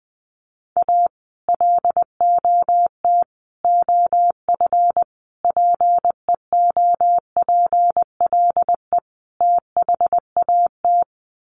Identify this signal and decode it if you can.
Morse code, heard as a keyed tone, about 19 wpm, 700 Hz. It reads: A LOT OF PEOPLE THAT
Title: Sets of 5 Words Encore2 20wpm